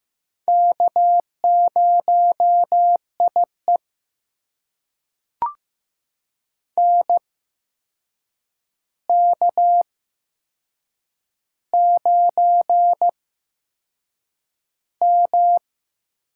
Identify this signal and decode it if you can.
Morse code, heard as a keyed tone, about 15 wpm, spaced as for 6 wpm, 700 Hz. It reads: K0IE NK9M
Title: USA Call Signs - 4 Characters - 8x Character Spacing - 15wpm